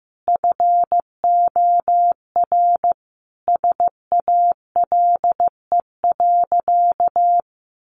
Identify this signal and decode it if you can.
Morse code, heard as a keyed tone, about 15 wpm, 700 Hz. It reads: FOR SALE.
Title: Sentences from Top 1400 Words 15wpm